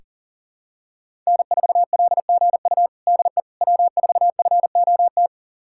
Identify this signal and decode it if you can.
Morse code, heard as a keyed tone, about 40 wpm, 700 Hz. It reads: N4LGU DE W4FOT
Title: USA Callsign DE Callsign - 5 Characters - 40wpm